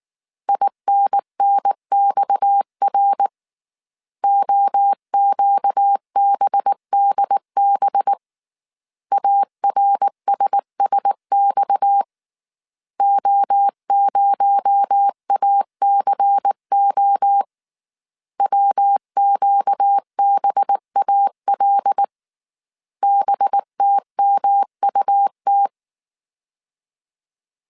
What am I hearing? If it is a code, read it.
Morse code, heard as a keyed tone, about 19 wpm, 800 Hz. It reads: INNXR OQBDB ARSSX O0ACO WQBAL BTMUT